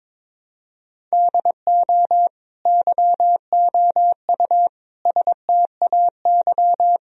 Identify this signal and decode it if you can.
Morse code, heard as a keyed tone, about 22 wpm, 700 Hz. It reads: DO YOU STAY
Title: Sentences from Top 1000 Words - Review Part 4 - 22wpm